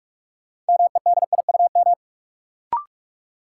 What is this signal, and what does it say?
Morse code, heard as a keyed tone, about 46 wpm, 700 Hz. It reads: MEDIUM
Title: Sentences from Top 900 Words - New Word Drill - 45wpm